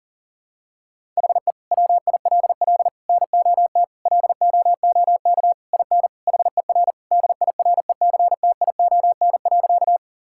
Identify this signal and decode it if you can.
Morse code, heard as a keyed tone, about 40 wpm, 700 Hz. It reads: HE WILL NOT LOOK IN HER DIRECTION.
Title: Sentences from Top 1000 Words - Review Part 5 - 40wpm